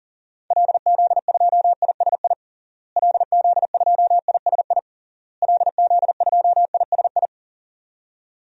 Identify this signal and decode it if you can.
Morse code, heard as a keyed tone, about 40 wpm, 700 Hz. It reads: LZ2ISI LZ2ISI LZ2ISI
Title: DXCC-100 Call Signs - 6 Characters - Repeated 3x - 40wpm